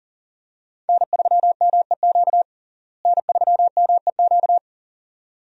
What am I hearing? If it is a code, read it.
Morse code, heard as a keyed tone, about 40 wpm, 700 Hz. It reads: N3MEQ N3MEQ
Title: USA Call Signs - 5 Characters - Repeated 2x - 40wpm